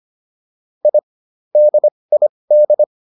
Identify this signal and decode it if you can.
Morse code, heard as a keyed tone, about 25 wpm, spaced as for 20 wpm, 600 Hz. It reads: I DID